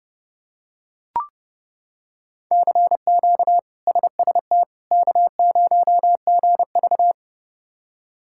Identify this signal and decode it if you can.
Morse code, heard as a keyed tone, about 31 wpm, 700 Hz. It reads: CQ SST K0GV
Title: K1USN SST QSO Practice - 30wpm